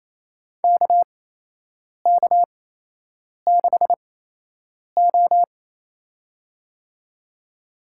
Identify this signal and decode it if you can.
Morse code, heard as a keyed tone, about 28 wpm, spaced as for 8 wpm, 700 Hz. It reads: KK6O
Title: USA Call Signs - 4 Characters - 8x Character Spacing - 28wpm